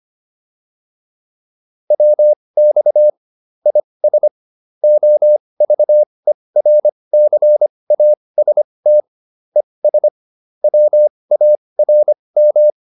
Morse code, heard as a keyed tone, about 25 wpm, spaced as for 20 wpm, 600 Hz. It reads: WX IS OVERCAST ES WARM